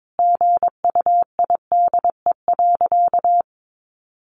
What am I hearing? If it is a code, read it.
Morse code, heard as a keyed tone, about 22 wpm, 700 Hz. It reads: GUIDE.
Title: Sentences from Top 600 Words 22wpm